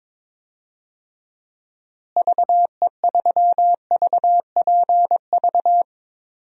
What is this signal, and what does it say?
Morse code, heard as a keyed tone, about 22 wpm, 700 Hz. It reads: VE3VPV
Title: DXCC-100 Call Signs - 6 Characters - Repeated 3x - 22wpm